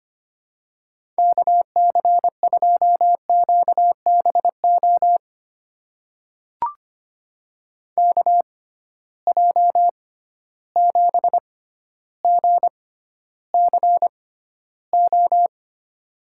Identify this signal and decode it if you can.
Morse code, heard as a keyed tone, about 25 wpm, spaced as for 13 wpm, 700 Hz. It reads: KC2QBO KJ7GCO
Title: USA Call Signs - 6 Characters - 6x Character Spacing - 25wpm